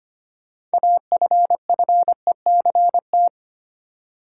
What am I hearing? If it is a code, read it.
Morse code, heard as a keyed tone, about 25 wpm, 700 Hz. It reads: AFFECT
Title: Six Letter Words - 25wpm